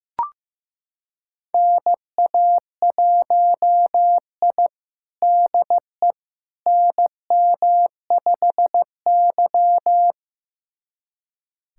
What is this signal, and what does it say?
Morse code, heard as a keyed tone, about 15 wpm, 700 Hz. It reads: NA1I DE NM5Y
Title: USA Callsign DE Callsign - 4 Characters - 15wpm